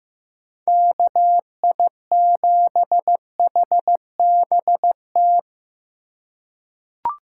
Morse code, heard as a keyed tone, about 15 wpm, 700 Hz. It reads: KI7HBT
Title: USA Call Signs - 6 Characters - 3x Character Spacing - 15wpm